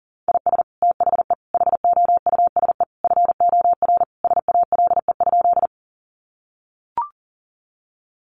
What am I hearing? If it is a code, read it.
Morse code, heard as a keyed tone, about 40 wpm, 700 Hz. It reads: IS THE HOUSE FOR SALE?
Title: Sentences from Top 1400 Words 40wpm